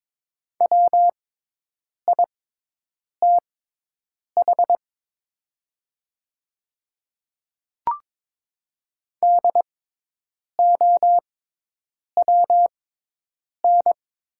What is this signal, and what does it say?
Morse code, heard as a keyed tone, about 22 wpm, spaced as for 8 wpm, 700 Hz. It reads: WITH DOWN
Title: 6x Character Spacing - Top 100 Words - 22wpm